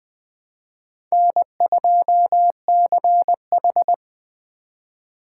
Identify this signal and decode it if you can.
Morse code, heard as a keyed tone, about 21 wpm, 700 Hz. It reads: N2CH
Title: Speed Racing - USA 1x2 Call Signs 17wpm